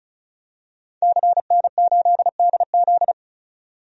Morse code, heard as a keyed tone, about 36 wpm, 700 Hz. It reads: CN8DZ
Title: DXCC-100 Call Signs - 5 Characters - 35wpm